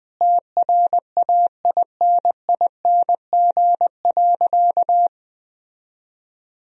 Morse code, heard as a keyed tone, about 21 wpm, 700 Hz. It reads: TRAINING.
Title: Sentences from Top 1000 Words - Review Part 4 - 20wpm